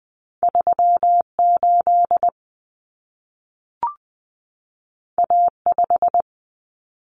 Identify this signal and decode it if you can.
Morse code, heard as a keyed tone, about 20 wpm, 700 Hz. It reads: 38 A5
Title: Common Cut Numbers - 2-Digit 20wpm